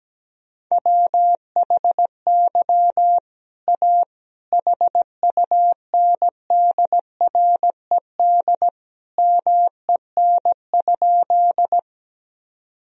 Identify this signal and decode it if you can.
Morse code, heard as a keyed tone, about 17 wpm, 700 Hz. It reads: WHY A HUNDRED MEN?
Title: Sentences from Top 1000 Words - Review Part 2 - 17wpm